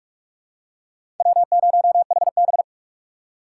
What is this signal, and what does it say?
Morse code, heard as a keyed tone, about 46 wpm, 700 Hz. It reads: W0HB
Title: USA Call Signs - 4 Characters - 4x Character Spacing - 45wpm